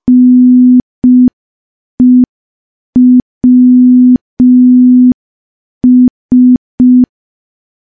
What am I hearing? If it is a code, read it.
Morse code, heard as a keyed tone, about 5 wpm, 250 Hz. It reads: NEWS
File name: morse / morse-training-05WPM